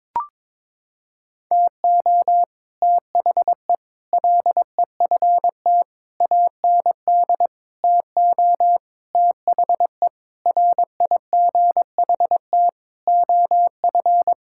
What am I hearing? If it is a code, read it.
Morse code, heard as a keyed tone, about 22 wpm, 700 Hz. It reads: TO THE LEFT AND TO THE RIGHT OF
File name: Sentences with 9-Word Phrases - 22wpm